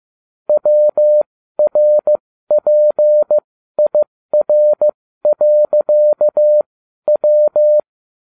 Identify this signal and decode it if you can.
Morse code, heard as a keyed tone, about 15 wpm, spaced as for 12 wpm, 600 Hz. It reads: WRPIR.W